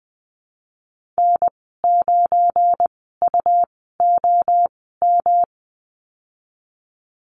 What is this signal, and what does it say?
Morse code, heard as a keyed tone, about 20 wpm, spaced as for 15 wpm, 700 Hz. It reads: N9UOM
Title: 173 Lesson 28g - Number 9 - Call Signs 20wpm